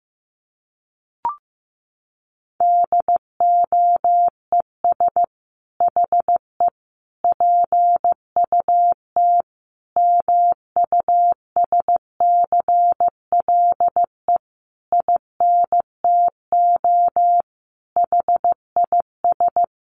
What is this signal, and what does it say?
Morse code, heard as a keyed tone, about 15 wpm, 700 Hz. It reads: DOES HE PUT MUSCLE INTO HIS
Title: Sentences from Top 1300 Words 15wpm